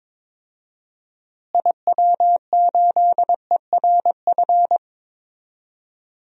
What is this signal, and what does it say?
Morse code, heard as a keyed tone, about 22 wpm, 700 Hz. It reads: IW8ERF
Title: DXCC-100 Call Signs - 6 Characters - Repeated 3x - 22wpm